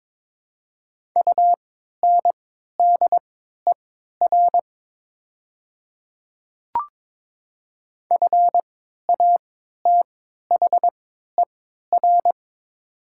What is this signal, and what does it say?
Morse code, heard as a keyed tone, about 23 wpm, spaced as for 13 wpm, 700 Hz. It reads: UNDER FATHER
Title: 3x Character Spacing - Top 200 Words - 22wpm